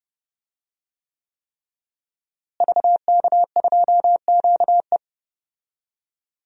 Morse code, heard as a keyed tone, about 30 wpm, 700 Hz. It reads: VK2QE
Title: DXCC-100 Call Signs - 5 Characters - Repeated 3x - 30wpm